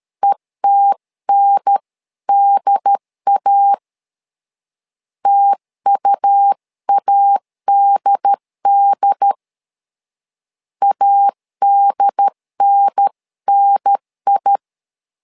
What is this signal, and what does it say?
Morse code, heard as a keyed tone, about 13 wpm, 800 Hz. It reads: ETNDA TUADD ADNNI
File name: LEZIONI / 2 UHDN / 2 UHDN 12wpm 1456